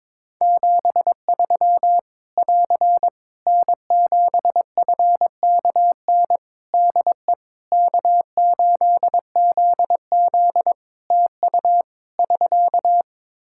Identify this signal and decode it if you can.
Morse code, heard as a keyed tone, about 22 wpm, 700 Hz. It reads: 73 <AR> N7FKN DE K8ZZ TU <SK>
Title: Practice QSO - Exchange 4 - Ending - 22wpm